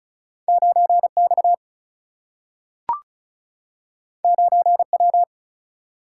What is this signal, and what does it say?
Morse code, heard as a keyed tone, about 36 wpm, 700 Hz. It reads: 9 X 9 W
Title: Single-Letter Single-Number 35wpm